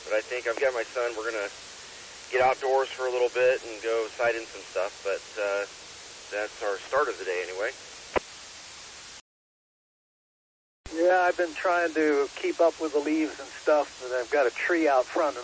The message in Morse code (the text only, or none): none